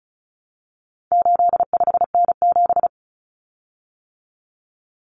Morse code, heard as a keyed tone, about 34 wpm, 700 Hz. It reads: 85N7
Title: Common Cut Numbers - 4-Digit 35wpm